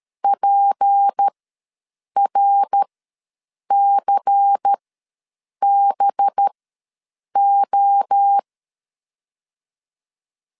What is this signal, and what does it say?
Morse code, heard as a keyed tone, about 13 wpm, spaced as for 7 wpm, 800 Hz. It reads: PRCBO
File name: LEZIONI / 6 PKX0 / 6 P ws 12wpm 3117